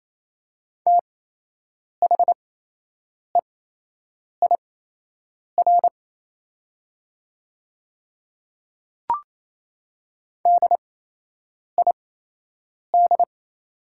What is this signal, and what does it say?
Morse code, heard as a keyed tone, about 28 wpm, spaced as for 8 wpm, 700 Hz. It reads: THEIR DID
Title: 8x Character Spacing - Top 200 Words - 28wpm